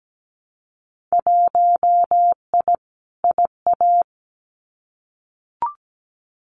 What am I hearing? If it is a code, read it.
Morse code, heard as a keyed tone, about 17 wpm, 700 Hz. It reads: 1I IA
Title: Winter Field Day Exchanges 17wpm